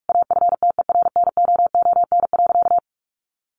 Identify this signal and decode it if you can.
Morse code, heard as a keyed tone, about 44 wpm, 700 Hz. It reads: AFTERNOON.
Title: Sentences from Top 1500 Words 45wpm